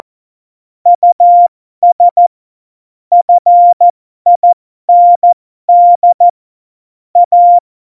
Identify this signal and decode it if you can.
Morse code, heard as a keyed tone, about 14 wpm, spaced as for 12 wpm, 700 Hz. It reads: US FIND A
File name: Morsekurs / Witze